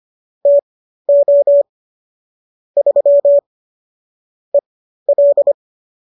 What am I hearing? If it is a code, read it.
Morse code, heard as a keyed tone, about 25 wpm, spaced as for 13 wpm, 570 Hz. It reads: TO 3 EL